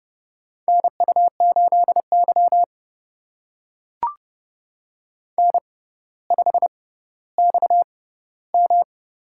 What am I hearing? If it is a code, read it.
Morse code, heard as a keyed tone, about 30 wpm, spaced as for 15 wpm, 700 Hz. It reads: NU8Y N5XM
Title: USA Call Signs - 4 Characters - 6x Character Spacing - 30wpm